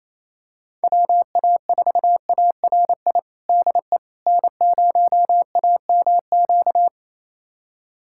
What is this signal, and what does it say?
Morse code, heard as a keyed tone, about 28 wpm, 700 Hz. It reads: WA4ARI DE N0AMQ
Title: USA Callsign DE Callsign 28wpm